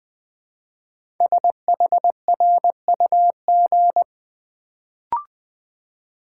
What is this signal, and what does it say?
Morse code, heard as a keyed tone, about 20 wpm, 700 Hz. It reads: SHRUG
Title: Build A Word - 1 Letter At A Time - 5-Letter Words - 20wpm